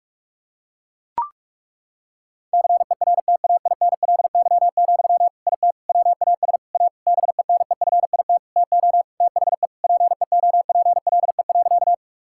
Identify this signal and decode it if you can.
Morse code, heard as a keyed tone, about 41 wpm, 700 Hz. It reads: CERTAINLY, IT WAS A BENEFIT TO THE PEOPLE.
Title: Sentences from Top 1000 Words - Review Part 6 - 45wpm